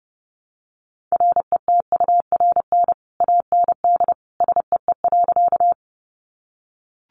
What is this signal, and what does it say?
Morse code, heard as a keyed tone, about 30 wpm, 700 Hz. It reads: RETURN AND SEE.
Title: Sentences from Top 800 Words 30wpm